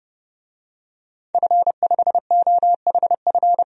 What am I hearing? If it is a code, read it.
Morse code, heard as a keyed tone, about 30 wpm, 700 Hz. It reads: F5OHF
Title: DXCC-100 Call Signs - 5 Characters - 30wpm